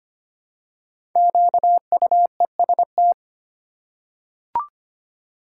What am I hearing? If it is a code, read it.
Morse code, heard as a keyed tone, about 25 wpm, 700 Hz. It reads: QUEST
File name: 5 Letter Words - First 3 Letters Given - 25wpm